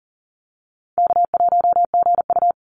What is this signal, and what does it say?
Morse code, heard as a keyed tone, about 37 wpm, 700 Hz. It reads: K1GU
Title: Easy Call Signs Repeated 3x - 40wpm